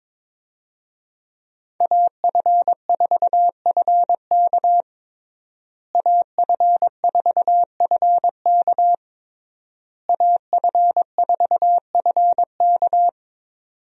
Morse code, heard as a keyed tone, about 21 wpm, 700 Hz. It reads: AF4FK AF4FK AF4FK
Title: USA Call Signs - 5 Characters - Repeated 3x - 22wpm